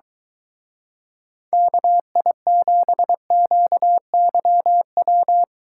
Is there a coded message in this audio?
Yes: Morse code from 1.5 to 5.4 s, heard as a keyed tone, about 23 wpm, 700 Hz. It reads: KI7QYW